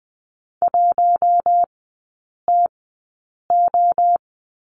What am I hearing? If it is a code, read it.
Morse code, heard as a keyed tone, about 20 wpm, spaced as for 8 wpm, 700 Hz. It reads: 1TO
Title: 041 Lesson 8e - Number 1 - Three Characters 20wpm